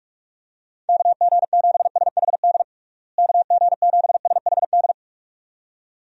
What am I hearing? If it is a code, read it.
Morse code, heard as a keyed tone, about 41 wpm, 700 Hz. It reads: KG7SHD KG7SHD
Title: USA Call Signs - 6 Characters - Repeated 2x - 45wpm